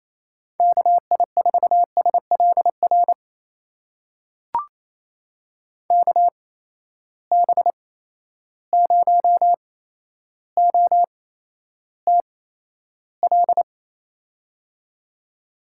Morse code, heard as a keyed tone, about 28 wpm, spaced as for 12 wpm, 700 Hz. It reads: KI4SLR KB0OTL
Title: USA Call Signs - 6 Characters - 8x Character Spacing - 28wpm